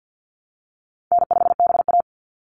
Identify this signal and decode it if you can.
Morse code, heard as a keyed tone, about 48 wpm, 700 Hz. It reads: N5BA
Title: Call Signs Repeated 3x - 50wpm